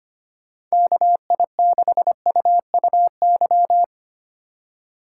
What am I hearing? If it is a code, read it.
Morse code, heard as a keyed tone, about 25 wpm, 700 Hz. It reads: KI6UUY
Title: USA Call Signs - 6 Characters - 3x Character Spacing - 25wpm